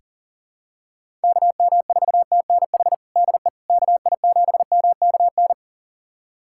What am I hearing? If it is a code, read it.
Morse code, heard as a keyed tone, about 37 wpm, 700 Hz. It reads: KM4TNH DE KI7MKN
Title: USA Callsign DE Callsign - 6 Characters - 40wpm